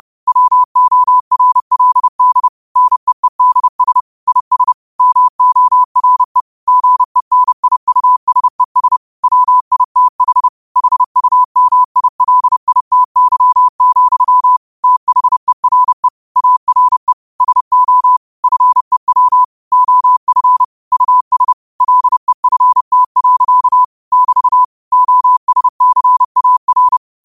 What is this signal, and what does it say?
Morse code, heard as a keyed tone, about 30 wpm, 1000 Hz. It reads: WORLD NEEDS IS MORE GENIUSES WITH HUMILITY, THERE ARE SO FEW OF US LEFT. <BT> OSCAR
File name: qotdfiles / 30wpm